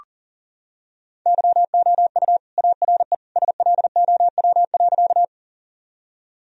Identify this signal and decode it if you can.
Morse code, heard as a keyed tone, about 37 wpm, 700 Hz. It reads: YOU ARE SLOW.
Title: Sentences from Top 1000 Words - Review Part 2 - 40wpm